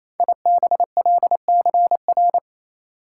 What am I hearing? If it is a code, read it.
Morse code, heard as a keyed tone, about 27 wpm, 700 Hz. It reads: IBLCR
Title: Code Groups - 5 Letter:Numbers - 28wpm